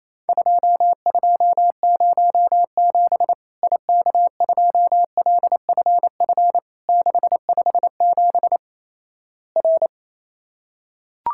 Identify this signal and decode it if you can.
Morse code, heard as a keyed tone, about 28 wpm, 700 Hz. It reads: 2207 IK2LFF 657 R
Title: Worked All Europe DX Contest - QTC Exchange - 28wpm